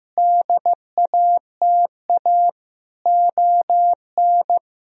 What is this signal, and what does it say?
Morse code, heard as a keyed tone, about 16 wpm, 700 Hz. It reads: DATA ON
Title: Sentences from Top 1000 Words - Review Part 3 - 15wpm